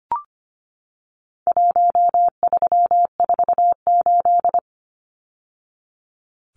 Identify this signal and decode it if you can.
Morse code, heard as a keyed tone, about 25 wpm, 700 Hz. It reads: 1348
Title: Numbers - Contest Style - 4-Digit 25wpm